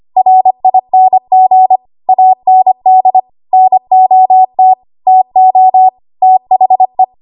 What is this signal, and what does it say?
Morse code, heard as a keyed tone, about 25 wpm, 750 Hz. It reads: RING AND NOT TO THE
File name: Archive / 25 WPM / 25WPM